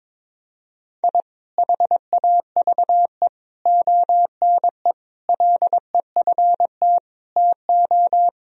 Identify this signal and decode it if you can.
Morse code, heard as a keyed tone, about 22 wpm, 700 Hz. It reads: I HAVE ONE LEFT TO
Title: Sentences from Top 1000 Words - Review Part 4 - 22wpm